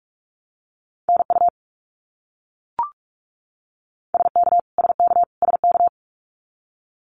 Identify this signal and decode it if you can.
Morse code, heard as a keyed tone, about 44 wpm, 700 Hz. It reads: NU SK SK SK
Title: Canadian Provinces and Territories - Repeated 3x - 45wpm